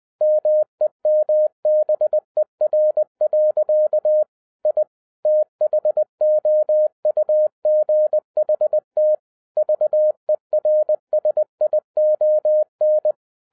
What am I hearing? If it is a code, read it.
Morse code, heard as a keyed tone, about 20 wpm, 610 Hz. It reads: MEMBER. I THOUGHT VERSION